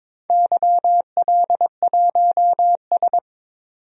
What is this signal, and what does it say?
Morse code, heard as a keyed tone, about 22 wpm, 700 Hz. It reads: YL1S
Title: DXCC-100 Call Signs - 4 Characters - 22wpm